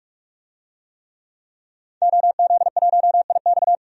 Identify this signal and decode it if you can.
Morse code, heard as a keyed tone, about 46 wpm, 700 Hz. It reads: OZ1IX
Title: DXCC-100 Call Signs - 5 Characters - 45wpm